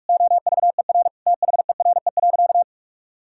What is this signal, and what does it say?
Morse code, heard as a keyed tone, about 46 wpm, 700 Hz. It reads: OVER THERE.
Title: Sentences from Top 1000 Words - Review Part 5 - 45wpm